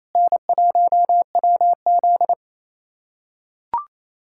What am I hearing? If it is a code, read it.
Morse code, heard as a keyed tone, about 28 wpm, 700 Hz. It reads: N1WZ
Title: USA Callsign DE Callsign - 4 Characters - 28wpm